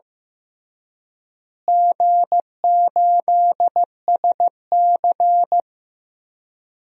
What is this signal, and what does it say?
Morse code, heard as a keyed tone, about 15 wpm, 700 Hz. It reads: G8SC
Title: DXCC-100 Call Signs - 4 Characters - Repeated 3x - 15wpm